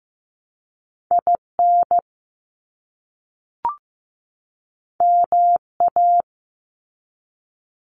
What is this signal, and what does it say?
Morse code, heard as a keyed tone, about 15 wpm, 700 Hz. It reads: IN MA
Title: US States 15wpm